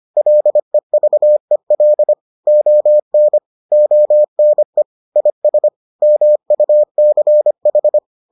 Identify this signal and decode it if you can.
Morse code, heard as a keyed tone, about 25 wpm, 600 Hz. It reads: LEVEL ON ONE IS MUCH